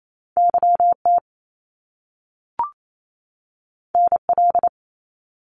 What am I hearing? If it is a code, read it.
Morse code, heard as a keyed tone, about 28 wpm, 700 Hz. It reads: YT NL
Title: Canadian Provinces and Territories 28wpm